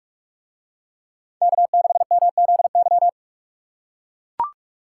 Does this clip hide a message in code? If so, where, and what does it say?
Yes: Morse code from 1.4 to 3.1 s, heard as a keyed tone, about 44 wpm, 700 Hz. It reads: K6MZY